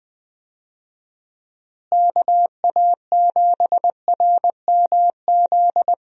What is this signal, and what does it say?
Morse code, heard as a keyed tone, about 21 wpm, 700 Hz. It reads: KA7RMZ
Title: USA Call Signs - 6 Characters - Repeated 2x - 20wpm